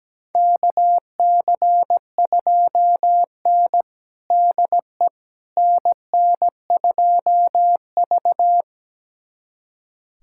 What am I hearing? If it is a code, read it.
Morse code, heard as a keyed tone, about 17 wpm, 700 Hz. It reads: KC2N DE NN2V
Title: USA Callsign DE Callsign - 4 Characters - 17wpm